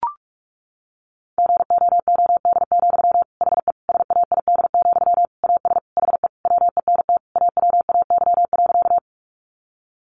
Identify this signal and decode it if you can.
Morse code, heard as a keyed tone, about 44 wpm, 700 Hz. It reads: GOOD, HE SAID, AS HE WENT AWAY.
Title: Sentences from Top 600 Words 45wpm